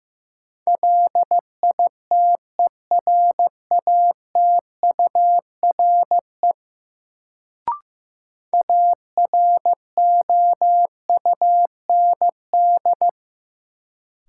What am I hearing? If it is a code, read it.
Morse code, heard as a keyed tone, about 15 wpm, 700 Hz. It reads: LITERATURE AROUND
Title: Sentences from Top 600 Words - New Word Drill - 15wpm